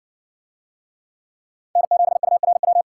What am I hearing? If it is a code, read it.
Morse code, heard as a keyed tone, about 60 wpm, 700 Hz. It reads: N7URW